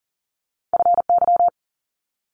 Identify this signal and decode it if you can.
Morse code, heard as a keyed tone, about 40 wpm, 700 Hz. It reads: FY
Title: Multi-letter Phonemes Quick Fire 40wpm